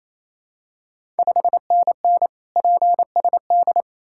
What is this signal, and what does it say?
Morse code, heard as a keyed tone, about 27 wpm, 690 Hz. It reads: 5NN PSD
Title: Ontario QSO Party Exchange 28wpm